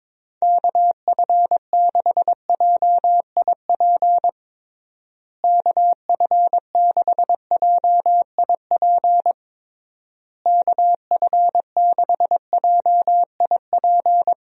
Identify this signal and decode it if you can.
Morse code, heard as a keyed tone, about 22 wpm, 700 Hz. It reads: KF6JIP KF6JIP KF6JIP
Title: USA Call Signs - 6 Characters - Repeated 3x - 22wpm